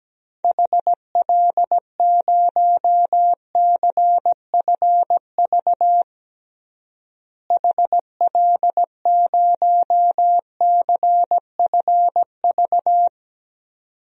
Morse code, heard as a keyed tone, about 17 wpm, 700 Hz. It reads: HL0CFV HL0CFV
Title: DXCC-100 Call Signs - 6 Characters - Repeated 2x - 17wpm